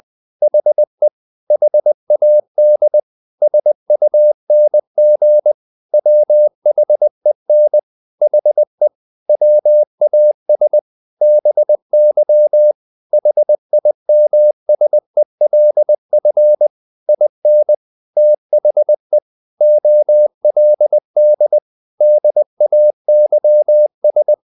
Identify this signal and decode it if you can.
Morse code, heard as a keyed tone, about 20 wpm, 600 Hz. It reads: HE HAD SUNG WHEN HE WAS BY HIMSELF IN THE OLD DAYS
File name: the old man and the sea / the old man and the sea part 1 20wpm